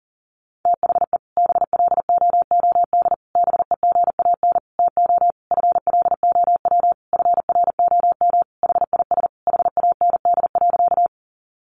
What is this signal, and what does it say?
Morse code, heard as a keyed tone, about 40 wpm, 700 Hz. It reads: THE BLOOD BEGAN TO FLOW FROM HIS HAND.
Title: Sentences from Top 1200 Words 40wpm